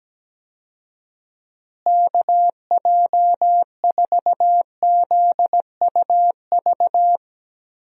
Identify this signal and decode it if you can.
Morse code, heard as a keyed tone, about 17 wpm, 700 Hz. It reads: KJ4ZUV
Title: USA Call Signs - 6 Characters - Repeated 2x - 17wpm